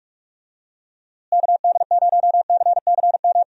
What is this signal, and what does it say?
Morse code, heard as a keyed tone, about 46 wpm, 700 Hz. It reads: KD0CCM
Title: USA Call Signs - 6 Characters - Repeated 2x - 45wpm